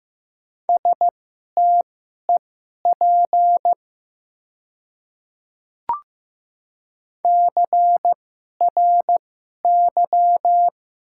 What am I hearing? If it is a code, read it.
Morse code, heard as a keyed tone, about 15 wpm, spaced as for 11 wpm, 700 Hz. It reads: STEP CRY
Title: 2x Character Spacing - Top 500 Words - 15wpm